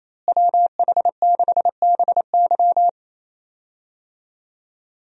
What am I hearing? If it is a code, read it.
Morse code, heard as a keyed tone, about 28 wpm, 700 Hz. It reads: WH6BY
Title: USA Call Signs - 5 Characters - Repeated 2x - 28wpm